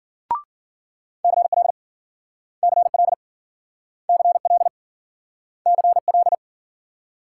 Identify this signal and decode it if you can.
Morse code, heard as a keyed tone, about 52 wpm, 700 Hz. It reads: CL CL CL CL